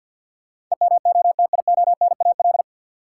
Morse code, heard as a keyed tone, about 52 wpm, 700 Hz. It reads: EMOTIONAL
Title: Sentences from Top 1000 Words - Review Part 5 - New Word Drill - 50wpm